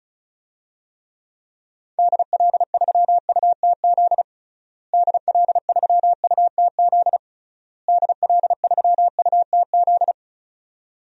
Morse code, heard as a keyed tone, about 34 wpm, 700 Hz. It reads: DL3UTZ DL3UTZ DL3UTZ
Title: DXCC-100 Call Signs - 6 Characters - Repeated 3x - 35wpm